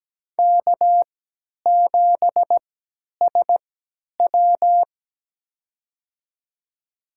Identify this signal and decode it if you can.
Morse code, heard as a keyed tone, about 17 wpm, spaced as for 10 wpm, 700 Hz. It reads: K7SW
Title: USA Call Signs - 4 Characters - 3x Character Spacing - 17wpm